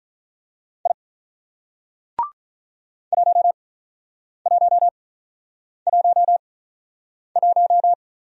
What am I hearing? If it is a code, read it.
Morse code, heard as a keyed tone, about 44 wpm, spaced as for 9 wpm, 700 Hz. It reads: I 1111
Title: Speed-Racing Single Letter-Number 35wpm